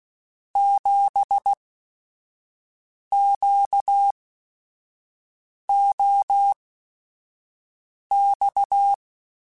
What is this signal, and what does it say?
Morse code, heard as a keyed tone, about 16 wpm, spaced as for 5 wpm, 800 Hz. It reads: 7QOX
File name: QSOs / QSOs-mp3s / WA7QOX 1@5